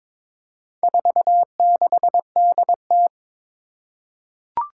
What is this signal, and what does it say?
Morse code, heard as a keyed tone, about 22 wpm, 700 Hz. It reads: 46DT
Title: Four Challenging Letter-Number Combinations 22wpm